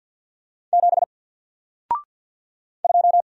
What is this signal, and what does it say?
Morse code, heard as a keyed tone, about 48 wpm, 700 Hz. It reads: 7 2
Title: Single Letter-Number Alt 50wpm